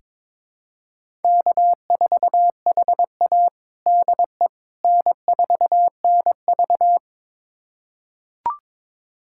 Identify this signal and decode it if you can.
Morse code, heard as a keyed tone, about 22 wpm, 700 Hz. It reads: K4HA DE N4NV
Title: USA Callsign DE Callsign - 4 Characters - 22wpm